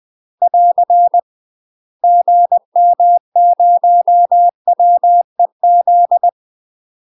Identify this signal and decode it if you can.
Morse code, heard as a keyed tone, about 20 wpm, 700 Hz. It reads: <AR> GM0WEZ
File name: UK MASTER QSOs TRAINING / 20WPM / UK MASTER QSOs 0068